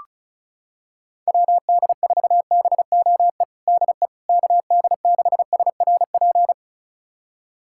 Morse code, heard as a keyed tone, about 36 wpm, 700 Hz. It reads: WD4BOE DE KD6SRP
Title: USA Callsign DE Callsign 35wpm